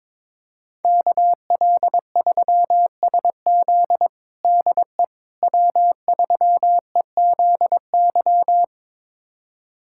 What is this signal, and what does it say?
Morse code, heard as a keyed tone, about 22 wpm, 700 Hz. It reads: KL3SZ DE W3EZY
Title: USA Callsign DE Callsign - 5 Characters - 22wpm